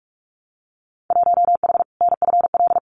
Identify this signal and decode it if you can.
Morse code, heard as a keyed tone, about 46 wpm, 700 Hz. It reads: 1H NFL
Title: Winter Field Day Exchanges 45wpm